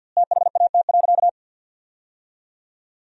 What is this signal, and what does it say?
Morse code, heard as a keyed tone, about 52 wpm, 700 Hz. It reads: THAT.
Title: 2x Word-Spacing - Sentences from Top 300 Words - 50wpm